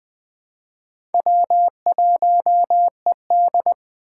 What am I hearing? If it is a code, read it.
Morse code, heard as a keyed tone, about 20 wpm, 700 Hz. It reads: W1ED
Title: USA Call Signs - 4 Characters - 4x Character Spacing - 20wpm